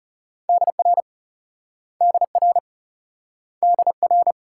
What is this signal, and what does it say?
Morse code, heard as a keyed tone, about 39 wpm, 700 Hz. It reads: DR DR DR